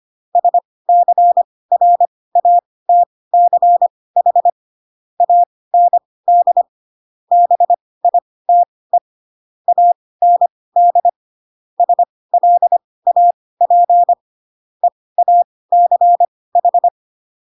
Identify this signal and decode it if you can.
Morse code, heard as a keyed tone, about 25 wpm, spaced as for 18 wpm, 700 Hz. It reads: SCRATCH AND BITE AND SLAP EACH
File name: Pinocchio / Pinocchio ch2 12 25 ik1hge